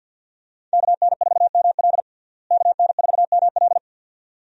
Morse code, heard as a keyed tone, about 53 wpm, 700 Hz. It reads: KN4ML KN4ML
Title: USA Call Signs - 5 Characters - Repeated 2x - 50wpm